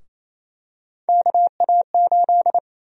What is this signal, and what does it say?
Morse code, heard as a keyed tone, about 28 wpm, 700 Hz. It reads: KA8
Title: USA Call Sign Prefixes - Repeated 2x - 28wpm